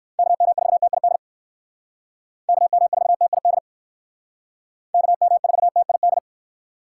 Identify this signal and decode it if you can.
Morse code, heard as a keyed tone, about 69 wpm, 700 Hz. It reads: KM4TID KM4TID KM4TID